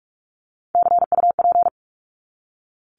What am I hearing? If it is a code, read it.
Morse code, heard as a keyed tone, about 44 wpm, 700 Hz. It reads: CUP
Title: Top 1501-1750 Words 45wpm